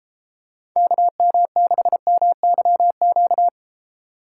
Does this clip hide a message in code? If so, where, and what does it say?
Yes: Morse code from 0.8 to 3.5 s, heard as a keyed tone, about 32 wpm, 700 Hz. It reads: KM6MYQ